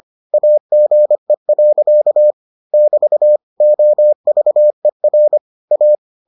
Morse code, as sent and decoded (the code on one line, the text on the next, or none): .- --. .-.-.- / -...- / --- ...- . .-. / .-
AGE. <BT> OVER A